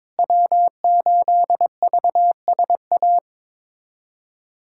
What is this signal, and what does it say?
Morse code, heard as a keyed tone, about 22 wpm, 700 Hz. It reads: W8VSA
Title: USA Callsign DE Callsign - 5 Characters - 22wpm